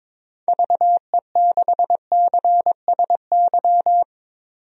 Morse code, heard as a keyed tone, about 22 wpm, 700 Hz. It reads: VE6CSY
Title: DXCC-100 Call Signs - 6 Characters - 22wpm